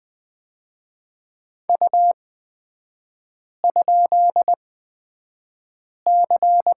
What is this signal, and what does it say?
Morse code, heard as a keyed tone, about 20 wpm, spaced as for 5 wpm, 700 Hz. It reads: U?C
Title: 250 Lesson 40e - BK - Three Characters 20wpm